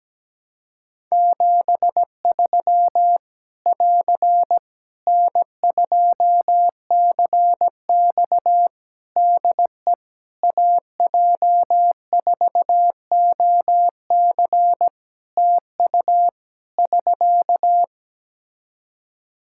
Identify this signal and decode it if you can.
Morse code, heard as a keyed tone, about 17 wpm, 700 Hz. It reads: 73 <AR> N2CX DE AJ4OC TU <SK>
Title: Practice QSO - Exchange 4 - Ending - 17wpm